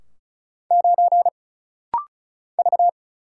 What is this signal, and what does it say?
Morse code, heard as a keyed tone, about 34 wpm, 700 Hz. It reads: 9 V
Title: Single Letter-Number - Rapid-Fire 35wpm